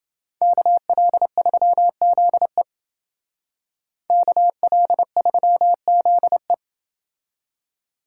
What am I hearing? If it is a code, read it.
Morse code, heard as a keyed tone, about 28 wpm, 700 Hz. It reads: KL3ZE KL3ZE